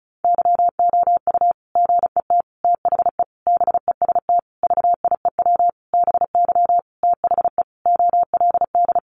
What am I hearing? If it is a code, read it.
Morse code, heard as a keyed tone, about 34 wpm, 700 Hz. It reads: YOU GET THE BEST VIEW BY THE OLD
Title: Sentences from Top 1200 Words 35wpm